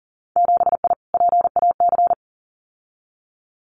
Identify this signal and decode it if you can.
Morse code, heard as a keyed tone, about 40 wpm, 700 Hz. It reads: 7I PAC
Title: Winter Field Day Exchanges Encore 40wpm